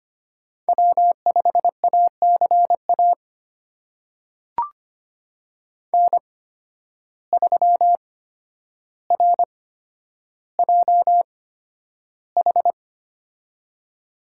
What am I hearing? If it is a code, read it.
Morse code, heard as a keyed tone, about 25 wpm, spaced as for 11 wpm, 700 Hz. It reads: W5ACA N3RJH
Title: USA Call Signs - 8x Character Spacing - 25wpm